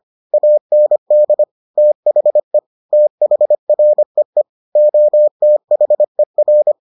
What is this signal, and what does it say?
Morse code, heard as a keyed tone, about 25 wpm, 600 Hz. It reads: AND THE THREE OTHER